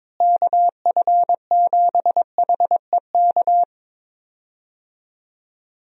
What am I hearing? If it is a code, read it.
Morse code, heard as a keyed tone, about 22 wpm, 700 Hz. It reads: KF7HEK
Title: USA Call Signs - 6 Characters - Repeated 3x - 22wpm